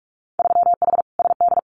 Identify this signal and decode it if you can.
Morse code, heard as a keyed tone, about 46 wpm, 700 Hz. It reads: 3H SD
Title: Winter Field Day Exchanges Encore 45wpm